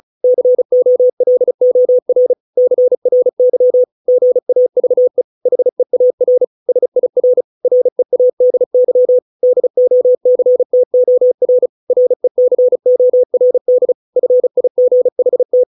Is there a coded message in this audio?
Yes: Morse code from 0.2 to 15.6 s, heard as a keyed tone, about 34 wpm, 500 Hz. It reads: COLOR CRY GAVE HEAR SIR READY DOCTOR RECORD FIGHT